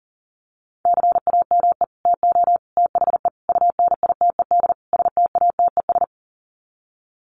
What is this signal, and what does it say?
Morse code, heard as a keyed tone, about 40 wpm, 700 Hz. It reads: CAME TO THE UNITED STATES
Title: Sets of 5 Words Encore2 40wpm